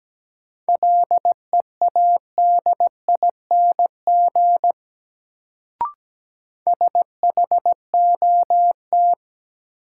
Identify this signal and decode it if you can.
Morse code, heard as a keyed tone, about 17 wpm, 700 Hz. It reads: LEADING SHOT
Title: Sentences from Top 1100 Words - New Word Drill Alt - 17wpm